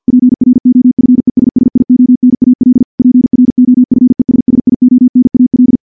Morse code, heard as a keyed tone, about 48 wpm, 250 Hz. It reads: PROFESSIONAL PROFESSIONAL
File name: morse / morse-training-50WPM